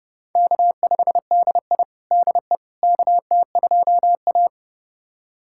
Morse code, heard as a keyed tone, about 30 wpm, 700 Hz. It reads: K5DI DE KT2A
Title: USA Callsign DE Callsign - 4 Characters - 30wpm